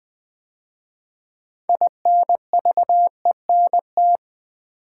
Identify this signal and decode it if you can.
Morse code, heard as a keyed tone, about 20 wpm, 700 Hz. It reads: INVENT
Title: 6 Letter Words - First 3 Letters Given - 20wpm